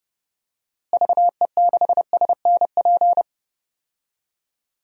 Morse code, heard as a keyed tone, about 30 wpm, 700 Hz. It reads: VE6SNP
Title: DXCC-100 Call Signs - 6 Characters - Repeated 2x - 30wpm